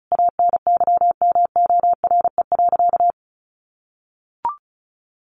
Morse code, heard as a keyed tone, about 34 wpm, 700 Hz. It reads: ANYMORE.
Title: Additional Easy Phrases 35wpm